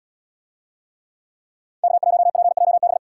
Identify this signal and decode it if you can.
Morse code, heard as a keyed tone, about 73 wpm, 700 Hz. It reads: K1PJB